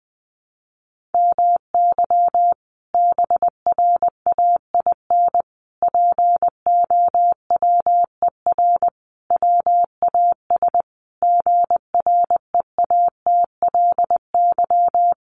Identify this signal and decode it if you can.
Morse code, heard as a keyed tone, about 20 wpm, 700 Hz. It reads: MY BRAIN POWER WAS GREATLY
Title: Sentences from Top 1400 Words 20wpm